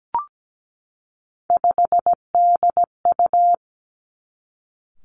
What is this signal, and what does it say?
Morse code, heard as a keyed tone, about 17 wpm, 700 Hz. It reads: 5DU
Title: Three Challenging Letter-Number Combinations 17wpm